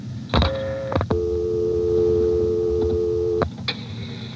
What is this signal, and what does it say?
Telephone call progress tone (North America), heard as a telephone network's tone, about 350, 440 Hz, with noise. Dial tone.